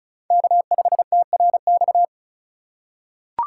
Morse code, heard as a keyed tone, about 36 wpm, 700 Hz. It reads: K5TRX
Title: USA Call Signs - 5 Characters - 3x Character Spacing - 35wpm